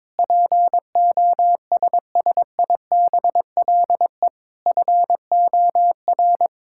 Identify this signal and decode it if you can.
Morse code, heard as a keyed tone, about 22 wpm, 700 Hz. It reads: POSSIBLE FOR
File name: practice / 22wpm / Sentences with 8-Word Phrases - 22wpm